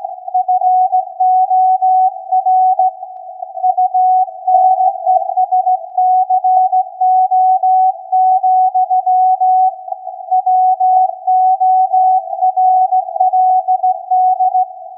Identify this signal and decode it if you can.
Morse code, heard as a keyed tone, about 16 wpm, 740 Hz. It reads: FOR UNESCO, WORLD
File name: iz7slz / VLF / SAQ 16-11-2022